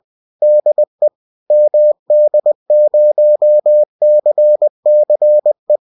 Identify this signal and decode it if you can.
Morse code, heard as a keyed tone, about 20 wpm, 600 Hz. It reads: DE MD0CCE